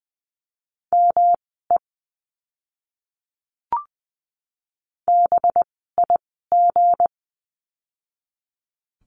Morse code, heard as a keyed tone, about 20 wpm, spaced as for 15 wpm, 700 Hz. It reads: ME BIG
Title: 192 Lesson 31f - Letter v - Words 20wpm